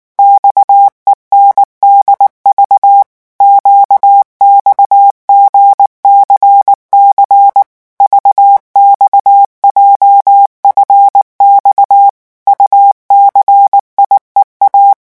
Morse code, heard as a keyed tone, about 19 wpm, 800 Hz. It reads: XENDV QXGCC VXJFX UCIEA
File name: morse mp3 / revision7-vitesse 18 mots